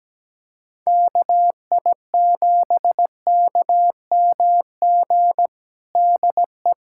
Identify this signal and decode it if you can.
Morse code, heard as a keyed tone, about 17 wpm, 700 Hz. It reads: KI7KMG DE
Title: USA Callsign DE Callsign - 6 Characters - 17wpm